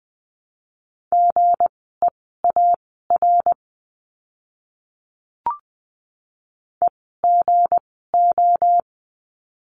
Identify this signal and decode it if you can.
Morse code, heard as a keyed tone, about 20 wpm, spaced as for 14 wpm, 700 Hz. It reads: GEAR EGO
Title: 160 Lesson 26f - Letter g - Words 20wpm